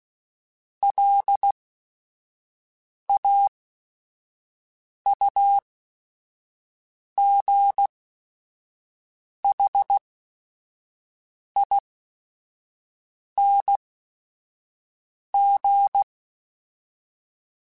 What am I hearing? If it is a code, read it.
Morse code, heard as a keyed tone, about 16 wpm, spaced as for 5 wpm, 800 Hz. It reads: LAUGHING
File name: Special INGs / Special INGs-mp3s / Sngl Ing @5